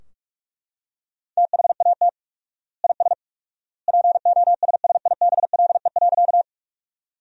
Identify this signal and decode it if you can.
Morse code, heard as a keyed tone, about 41 wpm, 700 Hz. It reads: THAT IS POSSIBLE.
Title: 4x Word-Spacing - Sentences from Top 500 Words - 45wpm